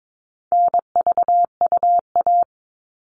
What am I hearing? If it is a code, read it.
Morse code, heard as a keyed tone, about 22 wpm, 700 Hz. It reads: NVUA
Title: All Cut Numbers - 4-Digit 22wpm